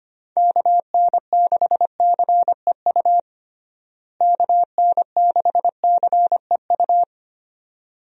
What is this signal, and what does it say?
Morse code, heard as a keyed tone, about 25 wpm, 700 Hz. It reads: KN6CEU KN6CEU
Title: USA Call Signs - 6 Characters - Repeated 2x - 25wpm